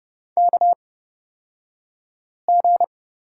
Morse code, heard as a keyed tone, about 30 wpm, spaced as for 5 wpm, 700 Hz. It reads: KG
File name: Lesson 38d - Letter q - Two Characters 30wpm